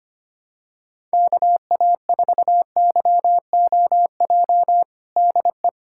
Morse code, heard as a keyed tone, about 25 wpm, 700 Hz. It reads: KA4YOJ DE
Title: USA Callsign DE Callsign 25wpm